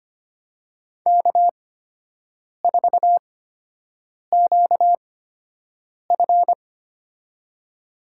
Morse code, heard as a keyed tone, about 25 wpm, spaced as for 7 wpm, 700 Hz. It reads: K4QF
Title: USA Call Signs - 4 Characters - 8x Character Spacing - 25wpm